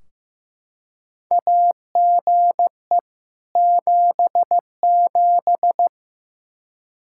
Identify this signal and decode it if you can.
Morse code, heard as a keyed tone, about 15 wpm, 700 Hz. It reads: AGE 77
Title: Practice QSO - Exchange 3 - More Info - 15wpm